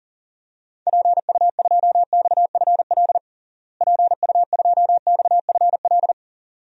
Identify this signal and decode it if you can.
Morse code, heard as a keyed tone, about 40 wpm, 700 Hz. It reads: PU2XFL PU2XFL
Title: DXCC-100 Call Signs - 6 Characters - Repeated 2x - 40wpm